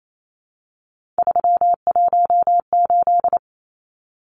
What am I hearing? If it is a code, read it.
Morse code, heard as a keyed tone, about 28 wpm, 700 Hz. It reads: 318
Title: Numbers - 3-Digit 28wpm